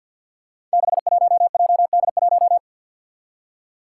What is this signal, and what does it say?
Morse code, heard as a keyed tone, about 52 wpm, 700 Hz. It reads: 61JD1
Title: Five Challenging Letter-Number Combinations 50wpm